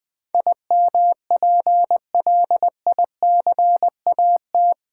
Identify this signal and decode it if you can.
Morse code, heard as a keyed tone, about 20 wpm, 700 Hz. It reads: IMPLICAT
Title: Build A Word - 4 Letters At A Time - 11-Letter Words - 20wpm